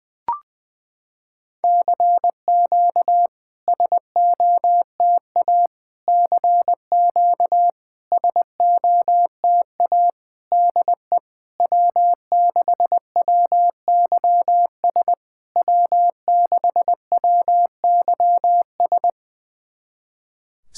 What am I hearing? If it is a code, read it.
Morse code, heard as a keyed tone, about 21 wpm, 700 Hz. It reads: CQ SOTA CQ SOTA DE W6WYS W6WYS
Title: SOTA QSO Practice 20wpm